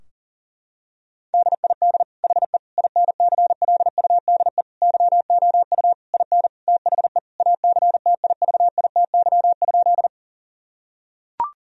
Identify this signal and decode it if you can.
Morse code, heard as a keyed tone, about 40 wpm, 700 Hz. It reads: DID HE INCLUDE YOU IN THE ACTIVITY?
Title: Sentences from Top 1000 Words - Review Part 4 - 40wpm